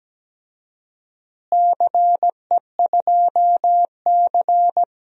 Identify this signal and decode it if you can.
Morse code, heard as a keyed tone, about 17 wpm, 700 Hz. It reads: CE2C
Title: DXCC-100 Call Signs - 4 Characters - Repeated 2x - 17wpm